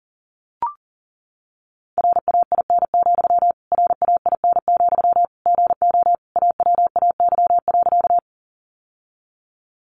Morse code, heard as a keyed tone, about 37 wpm, 700 Hz. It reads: RAIN, RAIN, GO AWAY.
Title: Sentences from Top 600 Words 40wpm